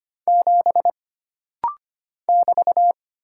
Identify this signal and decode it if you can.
Morse code, heard as a keyed tone, about 25 wpm, 700 Hz. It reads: 7 <BT>
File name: Single Character (Letter:Number:Punctuation:Prosign) - Rapid-Fire - 25wpm